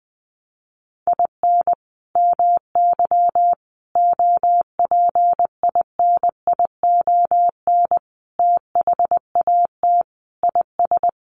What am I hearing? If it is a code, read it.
Morse code, heard as a keyed tone, about 20 wpm, 700 Hz. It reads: IN MY OPINION THAT IS
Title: Sentences from Top 1200 Words 20wpm